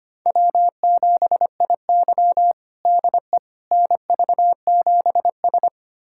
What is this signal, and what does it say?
Morse code, heard as a keyed tone, about 25 wpm, 700 Hz. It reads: W7IY DE NV7S
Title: USA Callsign DE Callsign - 4 Characters - 25wpm